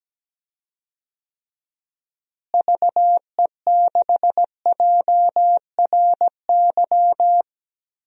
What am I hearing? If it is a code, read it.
Morse code, heard as a keyed tone, about 17 wpm, 700 Hz. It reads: VE6JRY
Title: DXCC-100 Call Signs - 6 Characters - Repeated 3x - 17wpm